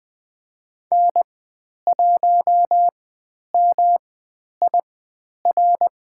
Morse code, heard as a keyed tone, about 20 wpm, spaced as for 10 wpm, 700 Hz. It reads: N1MIR
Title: 109 Lesson 18g - Letter m - Call Signs 20wpm